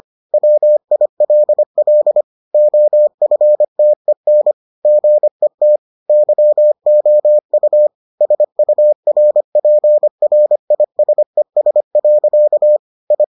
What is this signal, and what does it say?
Morse code, heard as a keyed tone, about 25 wpm, 600 Hz. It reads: WILL OFTEN GET YOU SURPRISES. I